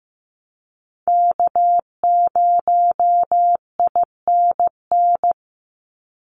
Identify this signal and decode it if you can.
Morse code, heard as a keyed tone, about 15 wpm, 700 Hz. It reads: K0INN
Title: Call Signs Repeated 2x - 15wpm